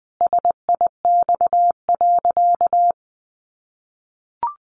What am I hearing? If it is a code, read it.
Morse code, heard as a keyed tone, about 19 wpm, 700 Hz. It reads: SIX.
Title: Sentences from Top 600 Words 20wpm